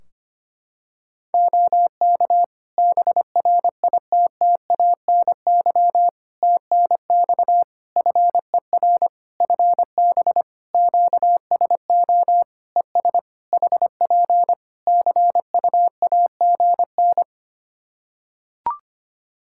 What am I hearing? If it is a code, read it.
Morse code, heard as a keyed tone, about 25 wpm, 700 Hz. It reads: OK BRITTANY TNX FER FB QSO ES HP CUAGN
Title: K1USN SST QSO Practice 25wpm